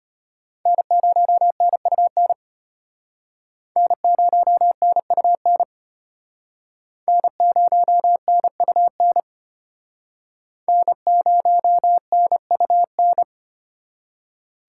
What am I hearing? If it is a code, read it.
Morse code, heard as a keyed tone, about 32 wpm, 700 Hz. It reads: N0NUN N0NUN N0NUN N0NUN